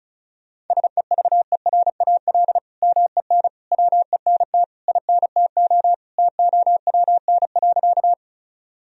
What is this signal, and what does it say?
Morse code, heard as a keyed tone, about 36 wpm, 700 Hz. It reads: SEVERAL MEN WENT INTO TOWN.
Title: Sentences from Top 1000 Words - Review Part 2 - 35wpm